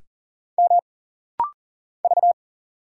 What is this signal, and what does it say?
Morse code, heard as a keyed tone, about 40 wpm, 700 Hz. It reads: M V
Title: Single Letter-Number Alt - Rapid-Fire 40wpm